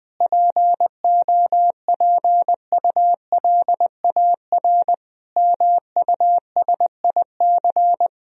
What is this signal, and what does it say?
Morse code, heard as a keyed tone, about 20 wpm, 700 Hz. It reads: POPULAR MUSIC
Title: Sentences from Top 1000 Words - Review Part 3 - 20wpm